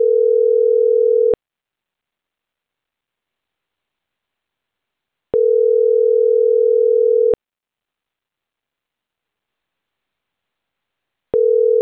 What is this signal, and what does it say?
Ringback tone (North America), 440 + 480 Hz.